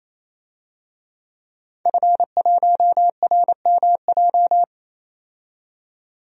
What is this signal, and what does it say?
Morse code, heard as a keyed tone, about 27 wpm, 700 Hz. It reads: F1RMJ
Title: DXCC-100 Call Signs - 5 Characters - 28wpm